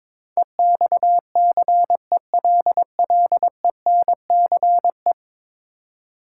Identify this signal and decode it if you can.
Morse code, heard as a keyed tone, about 22 wpm, 700 Hz. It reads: EXCELLENCE
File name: practice / 22wpm / 10 Letter Words - First 3 Letters Given - 22wpm